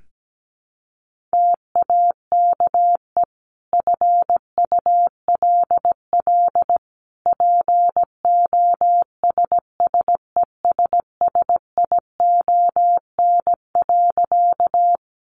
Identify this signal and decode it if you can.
Morse code, heard as a keyed tone, about 17 wpm, 700 Hz. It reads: TAKE FULL POSSESSION.
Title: Sentences from Top 1400 Words 17wpm